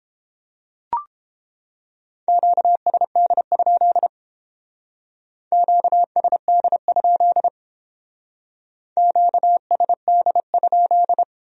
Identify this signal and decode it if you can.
Morse code, heard as a keyed tone, about 30 wpm, 700 Hz. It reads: QSD? QSD? QSD?